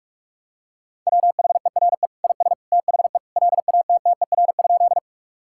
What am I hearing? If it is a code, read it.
Morse code, heard as a keyed tone, about 45 wpm, 700 Hz. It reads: WHERE IS THE LATTER?
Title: Sentences from Top 1600 Words 45wpm